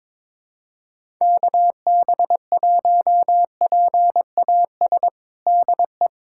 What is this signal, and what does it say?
Morse code, heard as a keyed tone, about 22 wpm, 700 Hz. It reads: KB1PAS DE
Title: USA Callsign DE Callsign 22wpm